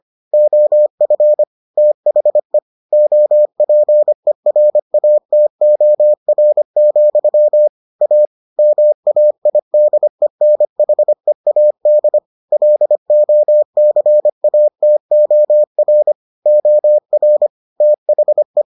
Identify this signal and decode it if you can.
Morse code, heard as a keyed tone, about 25 wpm, 600 Hz. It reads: OF THE OPERATOR, A MAIDENHEAD LOCATOR OR THE